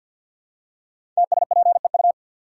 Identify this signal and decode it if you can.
Morse code, heard as a keyed tone, about 51 wpm, 700 Hz. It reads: TSPEU
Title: Code Groups - 5 Letter:Numbers - 50wpm